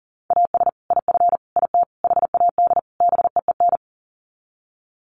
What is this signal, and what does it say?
Morse code, heard as a keyed tone, about 40 wpm, 700 Hz. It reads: AS IF IT HAD BEEN
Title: Sets of 5 Words Encore2 40wpm